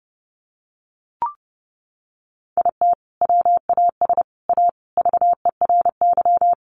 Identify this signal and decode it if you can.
Morse code, heard as a keyed tone, about 30 wpm, 700 Hz. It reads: IT WAS A VERY
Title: Sentences from Top 500 Words 30wpm